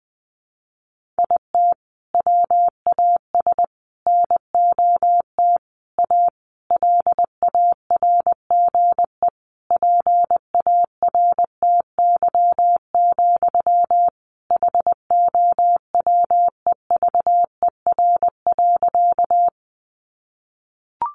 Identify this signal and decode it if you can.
Morse code, heard as a keyed tone, about 20 wpm, 700 Hz. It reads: IT WAS NOT A LARGE PARTY, HOWEVER.
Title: Sentences from Top 1400 Words 20wpm